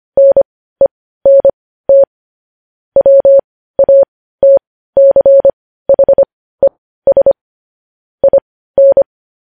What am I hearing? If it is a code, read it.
Morse code, heard as a keyed tone, about 25 wpm, spaced as for 15 wpm, 550 Hz. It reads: NENT WATCHES IN